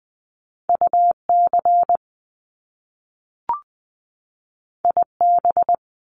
Morse code, heard as a keyed tone, about 20 wpm, 700 Hz. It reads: UC IB
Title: Top Two Letter Combinations in Words 20wpm